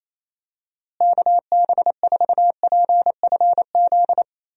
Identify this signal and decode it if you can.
Morse code, heard as a keyed tone, about 28 wpm, 700 Hz. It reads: KB4PFZ
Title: USA Call Signs - 6 Characters - Repeated 2x - 28wpm